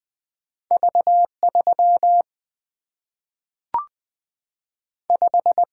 Morse code, heard as a keyed tone, about 20 wpm, 700 Hz. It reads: V 3 5
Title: Two Letter-Number Combinations - 20wpm